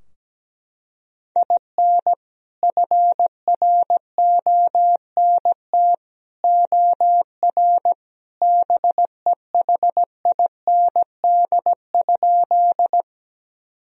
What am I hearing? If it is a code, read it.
Morse code, heard as a keyed tone, about 17 wpm, 700 Hz. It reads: IN FRONT OR BEHIND?
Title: Sentences from Top 1000 Words - Review Part 2 - 17wpm